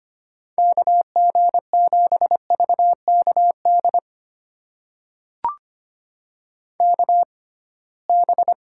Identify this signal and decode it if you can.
Morse code, heard as a keyed tone, about 25 wpm, spaced as for 19 wpm, 700 Hz. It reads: KG7VKD KB
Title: USA Call Signs - 6 Characters - 6x Character Spacing - 25wpm